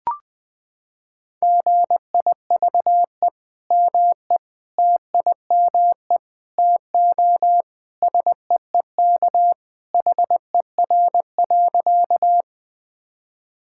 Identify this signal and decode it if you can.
Morse code, heard as a keyed tone, about 20 wpm, 700 Hz. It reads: GIVE ME TIME TO SEEK HER.
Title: Sentences from Top 1600 Words 20wpm